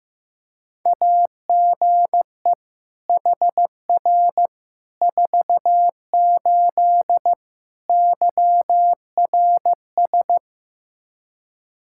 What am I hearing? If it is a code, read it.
Morse code, heard as a keyed tone, about 16 wpm, 700 Hz. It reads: AGE HR 48 YRS
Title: K1USN SST QSO Practice 15wpm